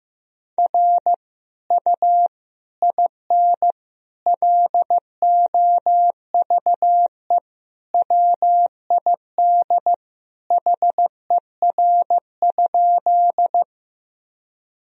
Morse code, heard as a keyed tone, about 15 wpm, 700 Hz. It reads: R U IN LOVE WID HER?
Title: Sentences from Top 400 Words and Abbreviations 15wpm